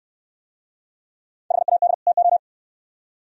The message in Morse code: HAD TO

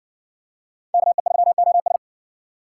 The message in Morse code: KE3OS

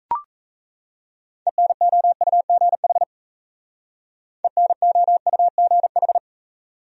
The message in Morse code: ENOUGH ENOUGH

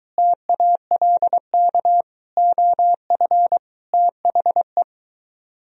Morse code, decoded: TALK OF THE

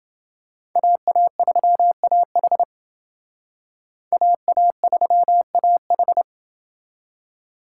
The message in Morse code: AA3AH AA3AH

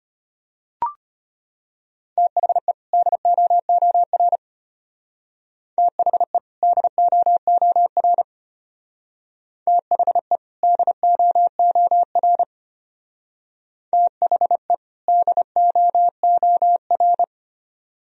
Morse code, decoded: THE DOOR THE DOOR THE DOOR THE DOOR

